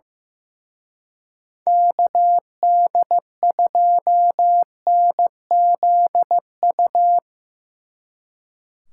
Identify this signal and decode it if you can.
Morse code, heard as a keyed tone, about 16 wpm, 700 Hz. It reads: KD2NZU
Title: Speed Racing - USA 2x3 Call Signs 15wpm